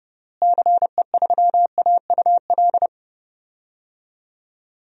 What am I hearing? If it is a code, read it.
Morse code, heard as a keyed tone, about 30 wpm, 700 Hz. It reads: CE3AUL
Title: DXCC-100 Call Signs - 6 Characters - Repeated 2x - 30wpm